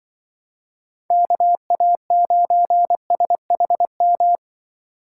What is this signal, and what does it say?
Morse code, heard as a keyed tone, about 24 wpm, 700 Hz. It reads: KA9SHM
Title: Speed Racing - USA 2x3 Call Signs 20wpm